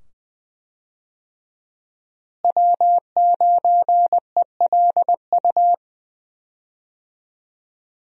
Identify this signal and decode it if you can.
Morse code, heard as a keyed tone, about 20 wpm, 700 Hz. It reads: W9ELU
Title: USA Call Signs - 5 Characters - Repeated 2x - 20wpm